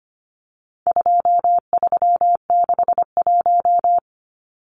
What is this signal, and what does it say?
Morse code, heard as a keyed tone, about 25 wpm, 700 Hz. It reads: 2361
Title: Numbers - 4-Digit 25wpm